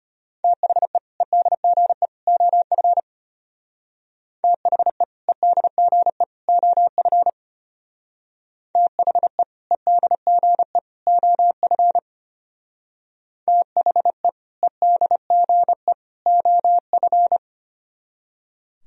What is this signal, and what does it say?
Morse code, heard as a keyed tone, about 32 wpm, 700 Hz. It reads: THE EDGE OF THE EDGE OF THE EDGE OF THE EDGE OF